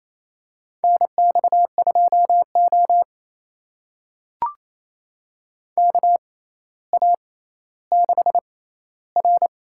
Morse code, heard as a keyed tone, about 27 wpm, spaced as for 14 wpm, 700 Hz. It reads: NX2O KA6R
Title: USA Call Signs - 4 Characters - 6x Character Spacing - 28wpm